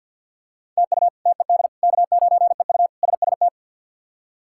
Morse code, heard as a keyed tone, about 52 wpm, 690 Hz. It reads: TU TED K9EU SST
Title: K1USN SST QSO Practice - 50wpm